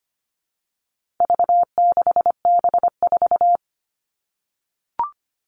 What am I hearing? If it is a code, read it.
Morse code, heard as a keyed tone, about 25 wpm, 700 Hz. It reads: V6B4
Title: All Cut Numbers - 4-Digit 25wpm